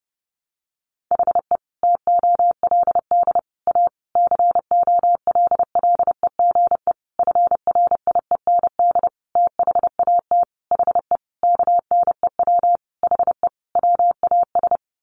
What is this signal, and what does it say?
Morse code, heard as a keyed tone, about 30 wpm, 700 Hz. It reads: HE TOLD A COLLEGE FRIEND THAT HE KNEW HE WAS
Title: Sentences from Top 1300 Words 30wpm